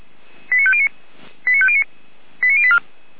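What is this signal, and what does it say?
Five-tone selective call (ZVEI), heard as quick runs of five tones, tones of about 70 ms. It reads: 89509 89509 89084
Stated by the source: ZVEI five-tone selective call received on an SDR.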